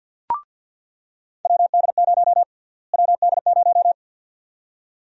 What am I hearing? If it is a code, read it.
Morse code, heard as a keyed tone, about 46 wpm, 700 Hz. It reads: WD0 WD0
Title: USA Call Sign Prefixes - Repeated 2x - 50wpm